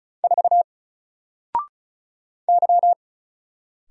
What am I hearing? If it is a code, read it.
Morse code, heard as a keyed tone, about 36 wpm, 700 Hz. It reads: 4 Y
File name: Single-Letter-Number-35wpm